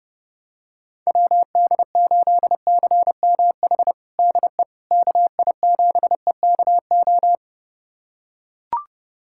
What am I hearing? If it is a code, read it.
Morse code, heard as a keyed tone, about 32 wpm, 700 Hz. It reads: WD8CMH DE KI7EKO
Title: USA Callsign DE Callsign 30wpm